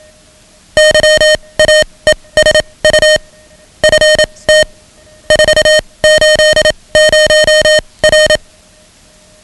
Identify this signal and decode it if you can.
Morse code, heard as a keyed tone, about 27 wpm, spaced as for 21 wpm, 620 Hz. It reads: YAESU FT 480R
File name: ham radio / CodeQuick / Code Quick 10